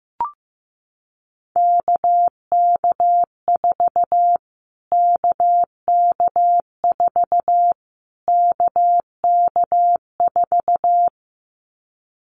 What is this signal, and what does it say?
Morse code, heard as a keyed tone, about 15 wpm, 700 Hz. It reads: KK4 KK4 KK4
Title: Callsign Prefixes for Top 25 DXCC - Repeated 3x - 15wpm